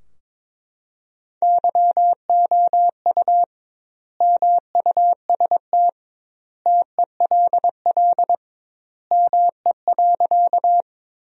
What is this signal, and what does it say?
Morse code, heard as a keyed tone, about 22 wpm, 700 Hz. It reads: YOU MUST TELL ME.
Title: 2x Word-Spacing - Sentences from Top 200 Words - 22wpm